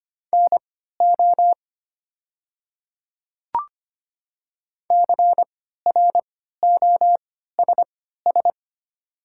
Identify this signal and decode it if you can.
Morse code, heard as a keyed tone, about 25 wpm, spaced as for 14 wpm, 700 Hz. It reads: NO CROSS
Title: 3x Character Spacing - Top 300 Words - 25wpm